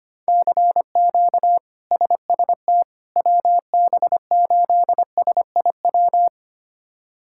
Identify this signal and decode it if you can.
Morse code, heard as a keyed tone, about 25 wpm, 700 Hz. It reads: CQ SST WB8SIW
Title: K1USN SST QSO Practice - 25wpm